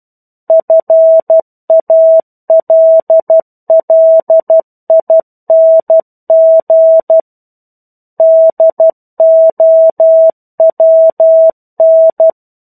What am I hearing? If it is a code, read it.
Morse code, heard as a keyed tone, about 12 wpm, 650 Hz. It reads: FALLING DOWN
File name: 01-Chapter-01 12 WPM